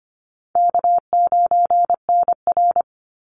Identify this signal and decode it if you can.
Morse code, heard as a keyed tone, about 25 wpm, 700 Hz. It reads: K9NR
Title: Easy Call Signs - 25wpm